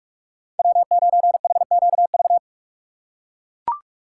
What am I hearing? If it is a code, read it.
Morse code, heard as a keyed tone, about 44 wpm, 700 Hz. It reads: W9HQV